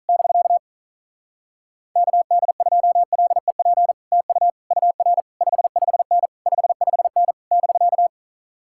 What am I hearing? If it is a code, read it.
Morse code, heard as a keyed tone, about 47 wpm, 700 Hz. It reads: <BK> KD2LEP TU UR 55N 55N <BK>